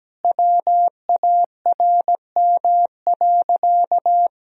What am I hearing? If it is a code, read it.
Morse code, heard as a keyed tone, about 17 wpm, 700 Hz. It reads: WARM.
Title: Sentences from Top 1000 Words - Review Part 2 - 17wpm